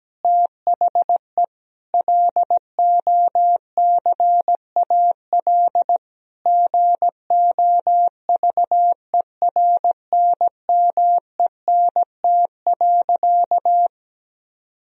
Morse code, heard as a keyed tone, about 17 wpm, 700 Hz. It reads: THE LOCAL GOVERNMENT.
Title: Sentences from Top 1000 Words - Review Part 3 - 17wpm